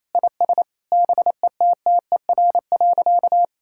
Morse code, heard as a keyed tone, about 28 wpm, 700 Hz. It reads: IS BETTER.
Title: Sentences from Top 1000 Words - Review Part 5 - 28wpm